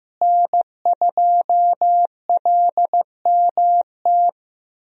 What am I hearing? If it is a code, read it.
Morse code, heard as a keyed tone, about 15 wpm, 700 Hz. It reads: N2LMT
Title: USA Call Signs - 5 Characters - Repeated 3x - 15wpm